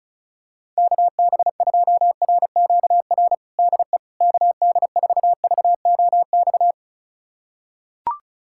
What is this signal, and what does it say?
Morse code, heard as a keyed tone, about 36 wpm, 700 Hz. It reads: KB2RQR DE KD4VOX
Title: USA Callsign DE Callsign - 6 Characters - 35wpm